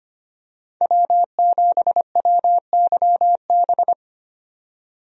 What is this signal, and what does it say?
Morse code, heard as a keyed tone, about 25 wpm, 700 Hz. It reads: W7WYB
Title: USA Call Signs - 5 Characters - 25wpm